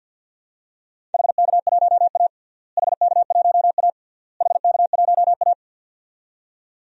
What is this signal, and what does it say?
Morse code, heard as a keyed tone, about 51 wpm, 700 Hz. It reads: HK1A HK1A HK1A
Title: DXCC-100 Call Signs - 4 Characters - Repeated 3x - 50wpm